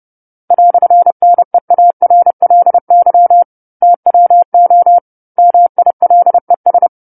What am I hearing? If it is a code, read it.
Morse code, heard as a keyed tone, about 30 wpm, 700 Hz. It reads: "NEARLY TWO MILES
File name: 10-Chapter-10 30 WPM